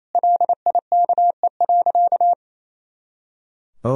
Morse code, heard as a keyed tone, about 28 wpm, 700 Hz. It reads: LIKE.
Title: Sentences from Top 500 Words and Abbreviations 28wpm